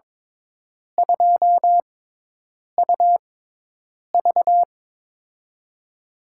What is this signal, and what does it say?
Morse code, heard as a keyed tone, about 22 wpm, spaced as for 8 wpm, 700 Hz. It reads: 2UV
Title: USA Call Signs - 5 Characters - 6x Character Spacing - 22wpm